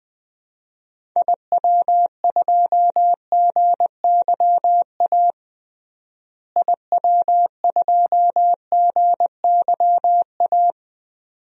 Morse code, heard as a keyed tone, about 20 wpm, 700 Hz. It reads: IW2GYA IW2GYA
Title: DXCC-100 Call Signs - 6 Characters - Repeated 2x - 20wpm